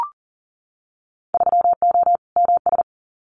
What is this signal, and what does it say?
Morse code, heard as a keyed tone, about 38 wpm, 700 Hz. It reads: 3O MS
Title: Winter Field Day Exchanges Encore 40wpm